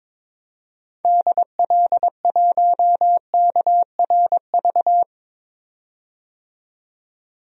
Morse code, heard as a keyed tone, about 22 wpm, 700 Hz. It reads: DL1KRV
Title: DXCC-100 Call Signs - 6 Characters - 22wpm